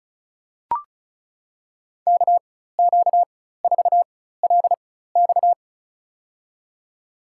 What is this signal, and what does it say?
Morse code, heard as a keyed tone, about 34 wpm, spaced as for 16 wpm, 700 Hz. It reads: KQ4LX
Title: USA Call Signs - 5 Characters - 4x Character Spacing - 35wpm